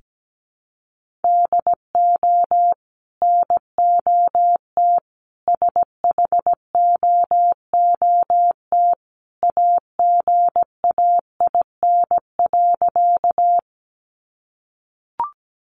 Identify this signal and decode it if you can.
Morse code, heard as a keyed tone, about 17 wpm, 700 Hz. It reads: DO NOT SHOOT AGAIN.
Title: Sentences from Top 1200 Words 17wpm